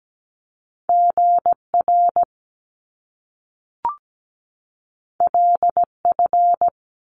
Morse code, heard as a keyed tone, about 17 wpm, 700 Hz. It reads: GR LF
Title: Top Two Letter Combinations in Words 17wpm